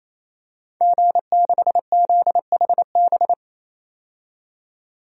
Morse code, heard as a keyed tone, about 27 wpm, 700 Hz. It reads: G6ZHB
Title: DXCC-100 Call Signs - 5 Characters - 28wpm